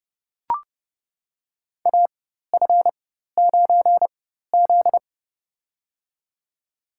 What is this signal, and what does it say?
Morse code, heard as a keyed tone, about 30 wpm, spaced as for 14 wpm, 700 Hz. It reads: AF9Z
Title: USA Call Signs - 4 Characters - 4x Character Spacing - 30wpm